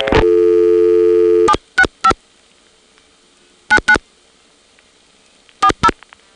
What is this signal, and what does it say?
DTMF, heard as telephone touch tones, tones of about 70 ms. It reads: 765 99 40